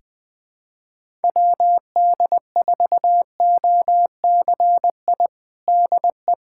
Morse code, heard as a keyed tone, about 20 wpm, 700 Hz. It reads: WD4OCI DE
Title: USA Callsign DE Callsign - 6 Characters - 20wpm